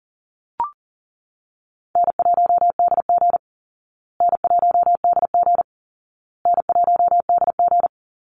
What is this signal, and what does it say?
Morse code, heard as a keyed tone, about 40 wpm, 700 Hz. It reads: N1DG N1DG N1DG
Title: Call Signs Repeated 2x - 40wpm